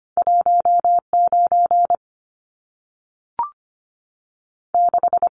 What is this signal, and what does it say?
Morse code, heard as a keyed tone, about 25 wpm, spaced as for 3 wpm, 700 Hz. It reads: 1 96
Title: Numbers - 2-Digit 25wpm